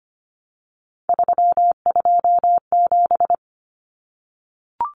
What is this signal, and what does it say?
Morse code, heard as a keyed tone, about 25 wpm, 700 Hz. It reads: 327
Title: Common Cut Numbers - 3-Digit 25wpm